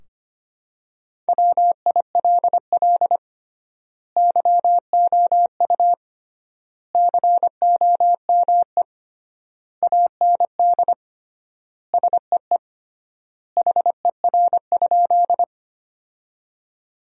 Morse code, heard as a keyed tone, about 25 wpm, 700 Hz. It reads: WILL YOU COME AND SEE HER?
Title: 3x Word-Spacing - Sentences from Top 100 Words - 25wpm